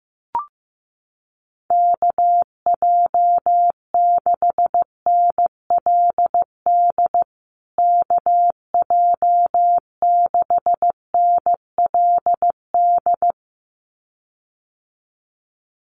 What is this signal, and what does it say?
Morse code, heard as a keyed tone, about 15 wpm, 700 Hz. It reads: KJ6NLD KJ6NLD
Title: USA Call Signs Repeated 2x 15wpm